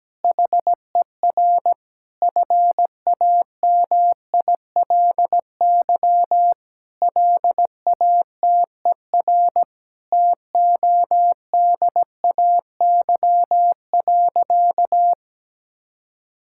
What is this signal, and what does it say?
Morse code, heard as a keyed tone, about 17 wpm, 700 Hz. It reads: HER FAMILY LATER TODAY.
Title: Sentences from Top 1600 Words 17wpm